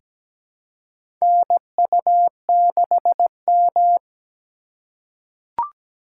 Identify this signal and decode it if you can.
Morse code, heard as a keyed tone, about 17 wpm, 700 Hz. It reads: NU6M
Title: USA Call Signs - 4 Characters - 3x Character Spacing - 17wpm